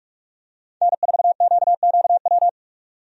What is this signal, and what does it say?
Morse code, heard as a keyed tone, about 46 wpm, 700 Hz. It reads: N4QQW
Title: USA Call Signs - 5 Characters - 45wpm